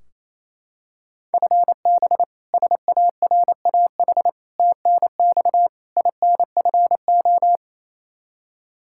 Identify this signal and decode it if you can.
Morse code, heard as a keyed tone, about 28 wpm, 700 Hz. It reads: FB SARAH TNX INFO
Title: K1USN SST QSO Practice 28wpm